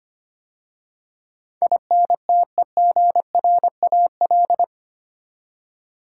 Morse code, heard as a keyed tone, about 25 wpm, 700 Hz. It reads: INTEGRAL
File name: 8 Letter Words - First 3 Letters Given - 25wpm